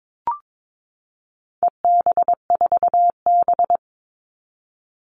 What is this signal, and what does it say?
Morse code, heard as a keyed tone, about 22 wpm, 700 Hz. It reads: EB4B
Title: All Cut Numbers - 4-Digit 22wpm